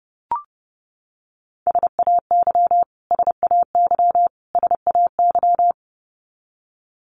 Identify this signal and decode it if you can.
Morse code, heard as a keyed tone, about 30 wpm, 700 Hz. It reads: SAY SAY SAY
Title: Top 500 Words Repeated 30wpm